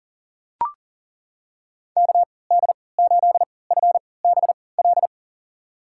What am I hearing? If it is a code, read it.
Morse code, heard as a keyed tone, about 40 wpm, spaced as for 23 wpm, 700 Hz. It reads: KD8FBL
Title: USA Call Signs - 3x Character Spacing - 40wpm